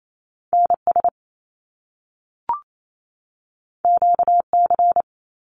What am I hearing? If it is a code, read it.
Morse code, heard as a keyed tone, about 28 wpm, 700 Hz. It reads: NS QC
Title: Canadian Provinces and Territories 28wpm